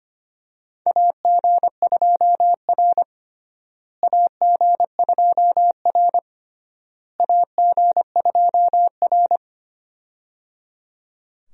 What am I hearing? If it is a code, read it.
Morse code, heard as a keyed tone, about 25 wpm, 700 Hz. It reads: AG2R AG2R AG2R
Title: USA Call Signs - 4 Characters - Repeated 3x - 25wpm